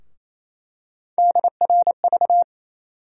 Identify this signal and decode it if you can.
Morse code, heard as a keyed tone, about 28 wpm, 700 Hz. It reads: DRV
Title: USA Call Sign Postfixes - 3 Letters 28wpm